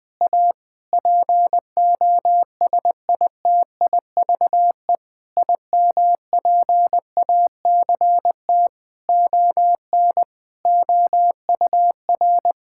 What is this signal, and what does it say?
Morse code, heard as a keyed tone, about 20 wpm, 700 Hz. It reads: A POSITIVE IMPACT ON OUR
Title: Sentences from Top 1000 Words - Review Part 6 - 20wpm